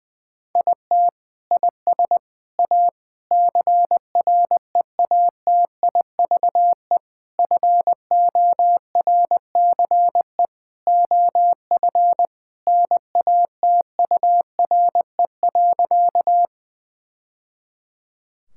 Morse code, heard as a keyed tone, about 21 wpm, 700 Hz. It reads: IT IS A CREATIVE FORCE OF NATURE.
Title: Sentences from Top 1000 Words - Review Part 5 - 20wpm